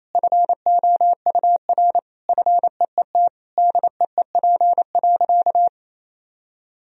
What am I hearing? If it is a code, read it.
Morse code, heard as a keyed tone, about 28 wpm, 700 Hz. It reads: FOUR FEET DEEP.
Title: Sentences from Top 1000 Words - Review Part 3 - 28wpm